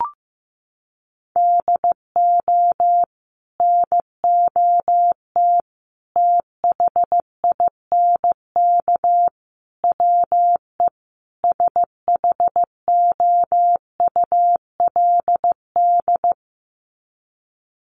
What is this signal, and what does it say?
Morse code, heard as a keyed tone, about 15 wpm, 700 Hz. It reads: DO NOT THINK WE SHOULD
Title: Sets of 5 Words Encore 15wpm